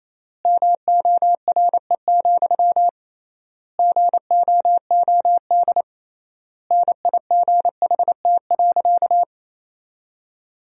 Morse code, heard as a keyed tone, about 28 wpm, 700 Hz. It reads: MORE, GOOD NIGHT.
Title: 3x Word-Spacing - Sentences from Top 300 Words - 28wpm